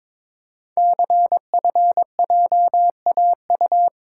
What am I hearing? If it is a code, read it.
Morse code, heard as a keyed tone, about 23 wpm, 700 Hz. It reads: CFJAU
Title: Code Groups - 5 Letters - 22wpm